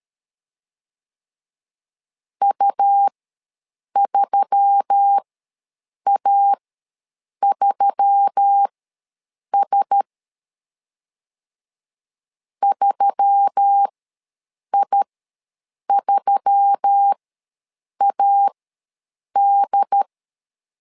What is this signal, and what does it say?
Morse code, heard as a keyed tone, about 13 wpm, spaced as for 7 wpm, 800 Hz. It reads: U3A3S 3I3AD
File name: LEZIONI / 4 Q1S3 ws 12wpm 2038